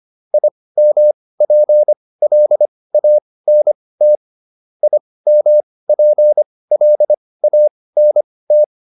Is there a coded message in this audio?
Yes: Morse code from 0.3 to 8.6 s, heard as a keyed tone, about 25 wpm, spaced as for 18 wpm, 600 Hz. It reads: IMPLANT IMPLANT